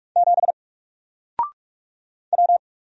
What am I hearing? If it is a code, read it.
Morse code, heard as a keyed tone, about 44 wpm, 700 Hz. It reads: 7 W
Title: Single Letter-Number Alt 45wpm